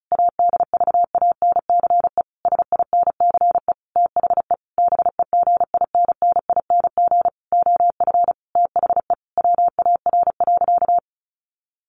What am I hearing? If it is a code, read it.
Morse code, heard as a keyed tone, about 34 wpm, 700 Hz. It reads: ADVANCE SINCE THE BEGINNING OF THE WAR.
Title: Sentences from Top 1400 Words 35wpm